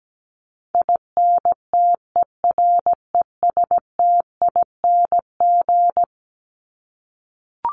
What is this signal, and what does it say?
Morse code, heard as a keyed tone, about 17 wpm, 700 Hz. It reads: INTERESTING
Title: ING Suffix - Top 100 Words with Base Words 17wpm